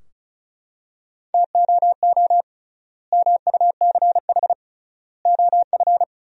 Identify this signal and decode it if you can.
Morse code, heard as a keyed tone, about 34 wpm, 700 Hz. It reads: TOO MUCH OF
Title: 3x Word-Spacing - Sentences from Top 200 Words - 35wpm